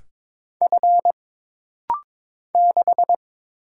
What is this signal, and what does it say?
Morse code, heard as a keyed tone, about 22 wpm, 700 Hz. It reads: F 6
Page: This practice set is composed of Letters, Numbers, common punctuations, and common prosigns.